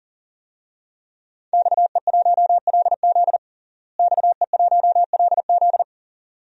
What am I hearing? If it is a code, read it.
Morse code, heard as a keyed tone, about 37 wpm, 700 Hz. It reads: XE1LZ XE1LZ
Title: DXCC-100 Call Signs - 5 Characters - Repeated 2x - 40wpm